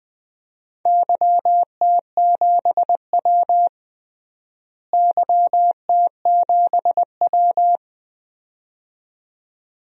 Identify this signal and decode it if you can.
Morse code, heard as a keyed tone, about 20 wpm, 700 Hz. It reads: YT7W YT7W
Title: DXCC-100 Call Signs - 4 Characters - Repeated 2x - 20wpm